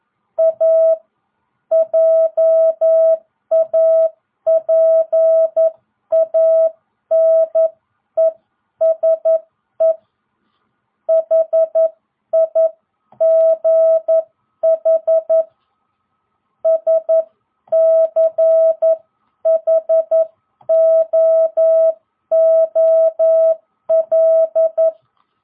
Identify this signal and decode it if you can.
Morse code, heard as a keyed tone, about 11 wpm, 660 Hz. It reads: A JAPANESE HIGH SCHOOL